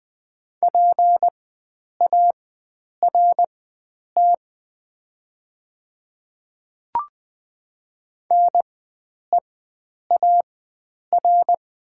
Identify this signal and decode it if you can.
Morse code, heard as a keyed tone, about 20 wpm, spaced as for 9 wpm, 690 Hz. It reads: PART NEAR
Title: 4x Character Spacing - Top 300 Words - 20wpm